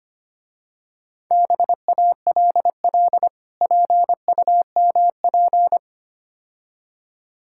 Morse code, heard as a keyed tone, about 27 wpm, 700 Hz. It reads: BALL PUMP
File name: practice / 25wpm / Difficult Words - Speed-Racing with Hints - 25wpm